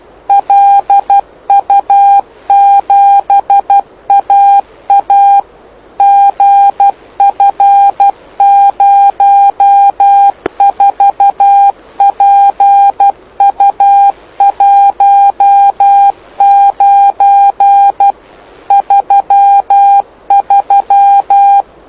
Morse code, heard as a keyed tone, about 12 wpm, 800 Hz. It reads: LU7AA GF04PU19 33
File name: lu4aao / lu7aa 09dic2011 exp planeador vuelo1